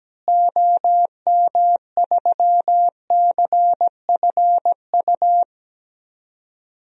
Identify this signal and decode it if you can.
Morse code, heard as a keyed tone, about 17 wpm, 700 Hz. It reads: OM3CFU
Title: DXCC-100 Call Signs - 6 Characters - Repeated 3x - 17wpm